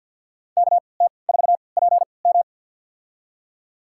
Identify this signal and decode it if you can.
Morse code, heard as a keyed tone, about 53 wpm, spaced as for 29 wpm, 700 Hz. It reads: KT4PM
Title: USA Call Signs - 3x Character Spacing - 50wpm